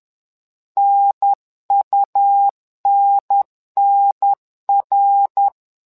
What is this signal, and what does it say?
Morse code, heard as a keyed tone, about 11 wpm, 800 Hz. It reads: NUNNR
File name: LEZIONI / 3 MWRO / 3 MWRO 10wpm 2148